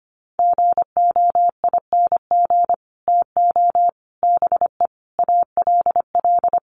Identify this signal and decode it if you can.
Morse code, heard as a keyed tone, about 25 wpm, 700 Hz. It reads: GOING TO BE ALL
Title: Sets of 5 Words Encore2 25wpm